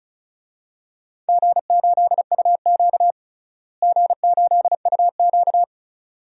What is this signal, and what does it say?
Morse code, heard as a keyed tone, about 34 wpm, 690 Hz. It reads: G8UQ G8UQ
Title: DXCC-100 Call Signs - 4 Characters - Repeated 2x - 35wpm